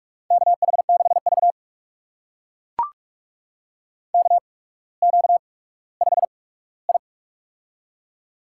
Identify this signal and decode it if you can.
Morse code, heard as a keyed tone, about 44 wpm, spaced as for 19 wpm, 700 Hz. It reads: KH6V KQ5I
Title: USA Call Signs - 8x Character Spacing - 45wpm